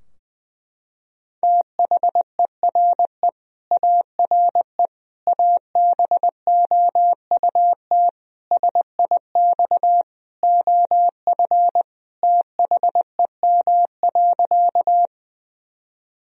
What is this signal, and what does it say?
Morse code, heard as a keyed tone, about 20 wpm, 700 Hz. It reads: THERE ARE ABOUT SIX OF THEM.
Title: Sentences from Top 1000 Words - Review Part 2 - 20wpm